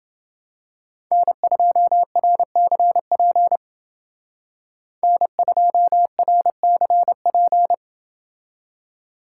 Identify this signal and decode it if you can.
Morse code, heard as a keyed tone, about 29 wpm, 700 Hz. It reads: N2RCP N2RCP